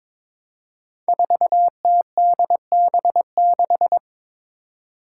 Morse code, heard as a keyed tone, about 22 wpm, 700 Hz. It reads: 4TDB6
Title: Five Challenging Letter-Number Combinations 22wpm